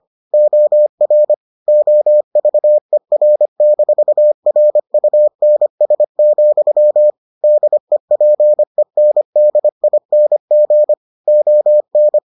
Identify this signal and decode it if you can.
Morse code, heard as a keyed tone, about 25 wpm, 600 Hz. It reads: OR OVER-RUNS, DEPENDING ON